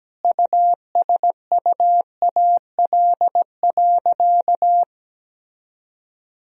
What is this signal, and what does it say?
Morse code, heard as a keyed tone, about 17 wpm, 700 Hz. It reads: USUAL.
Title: Sentences from Top 1000 Words - Review Part 2 - 17wpm